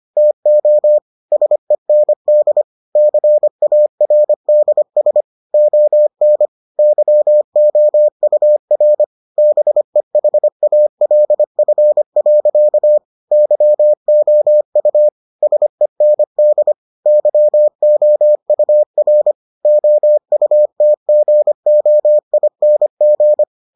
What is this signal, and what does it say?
Morse code, heard as a keyed tone, about 25 wpm, 600 Hz. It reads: TO SEND CARDS ON YOUR BEHALF. YOU SEND YOUR OUTGOING